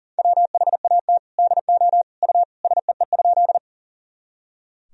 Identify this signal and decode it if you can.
Morse code, heard as a keyed tone, about 40 wpm, 700 Hz. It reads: WHAT DO U SEE?
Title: Sentences from Top 100 Words and Abbreviations 40wpm